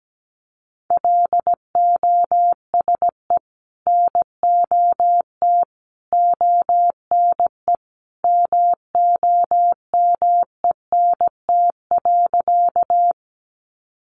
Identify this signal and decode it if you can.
Morse code, heard as a keyed tone, about 17 wpm, 700 Hz. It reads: LOSE NOT ONE MOMENT.
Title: Sentences from Top 1100 Words 17wpm